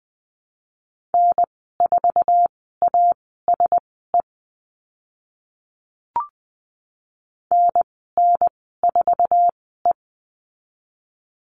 Morse code, heard as a keyed tone, about 20 wpm, spaced as for 14 wpm, 700 Hz. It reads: N4ASE NN4E
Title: 055 Lesson 10g - Letter r - Call Signs 20wpm